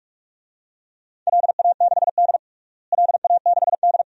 Morse code, heard as a keyed tone, about 46 wpm, 700 Hz. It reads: LA6D LA6D
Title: DXCC-100 Call Signs - 4 Characters - Repeated 2x - 45wpm